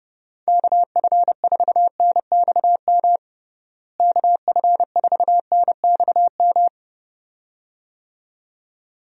Morse code, heard as a keyed tone, about 30 wpm, 700 Hz. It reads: KF4NXM KF4NXM
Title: USA Call Signs - 6 Characters - Repeated 2x - 30wpm